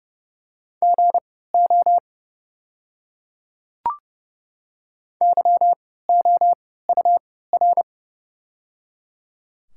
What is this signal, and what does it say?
Morse code, heard as a keyed tone, about 30 wpm, spaced as for 17 wpm, 700 Hz. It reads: GO YOUR
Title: 3x Character Spacing - Top 100 Words - 30wpm